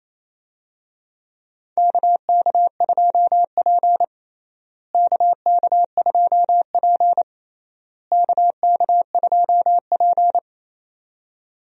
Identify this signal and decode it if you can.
Morse code, heard as a keyed tone, about 28 wpm, 700 Hz. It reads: KK2P KK2P KK2P
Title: USA Call Signs - 4 Characters - Repeated 3x - 28wpm